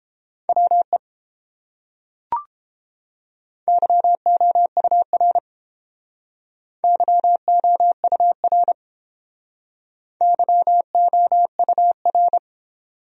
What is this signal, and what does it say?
Morse code, heard as a keyed tone, about 31 wpm, 700 Hz. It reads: WE YOUR YOUR YOUR